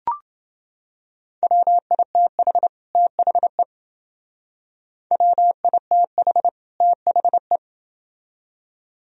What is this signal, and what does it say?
Morse code, heard as a keyed tone, about 28 wpm, 700 Hz. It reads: WITH THE WITH THE